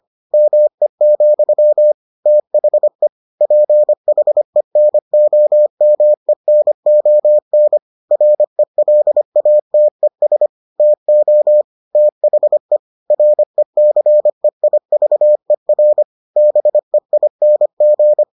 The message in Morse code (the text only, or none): ME, THE PHENOMENON RELATES TO THE RECEIVER BEING